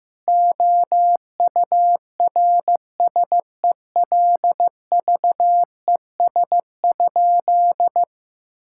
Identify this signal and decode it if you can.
Morse code, heard as a keyed tone, about 16 wpm, 700 Hz. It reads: OURSELVES?
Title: Sentences from Top 1600 Words 15wpm